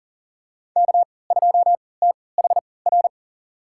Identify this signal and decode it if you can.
Morse code, heard as a keyed tone, about 40 wpm, spaced as for 23 wpm, 700 Hz. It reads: K2THR
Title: USA Call Signs - 5 Characters - 3x Character Spacing - 40wpm